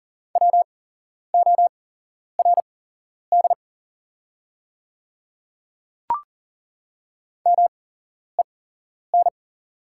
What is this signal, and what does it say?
Morse code, heard as a keyed tone, about 37 wpm, spaced as for 11 wpm, 690 Hz. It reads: WORD MEN
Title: 8x Character Spacing - Top 300 Words - 40wpm